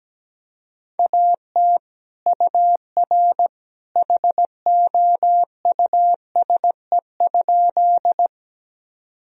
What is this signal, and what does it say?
Morse code, heard as a keyed tone, about 17 wpm, 700 Hz. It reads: AT UR HOUSE?
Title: Sentences from Top 200 Words and Abbreviations 17wpm